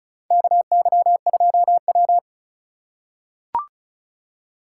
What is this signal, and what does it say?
Morse code, heard as a keyed tone, about 36 wpm, 700 Hz. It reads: KY2W
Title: USA Call Signs - 4x Character Spacing - 35wpm